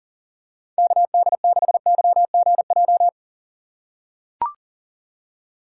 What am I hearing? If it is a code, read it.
Morse code, heard as a keyed tone, about 40 wpm, 700 Hz. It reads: KD6YGJ
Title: USA Call Signs - 3x Character Spacing - 40wpm